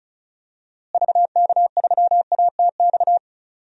Morse code, heard as a keyed tone, about 34 wpm, 700 Hz. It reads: VK3ATX
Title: DXCC-100 Call Signs - 6 Characters - Repeated 3x - 35wpm